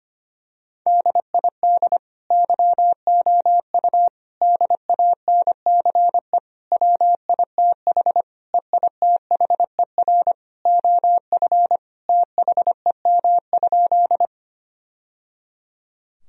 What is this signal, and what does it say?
Morse code, heard as a keyed tone, about 25 wpm, 700 Hz. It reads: DID YOU DANCE WITH EITHER OF THEM?
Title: Sentences from Top 1000 Words - Review Part 4 - 25wpm